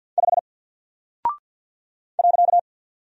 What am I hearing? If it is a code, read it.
Morse code, heard as a keyed tone, about 50 wpm, 700 Hz. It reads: 5 .
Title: Single Character (Letter:Number:Punctuation:Prosign) 50wpm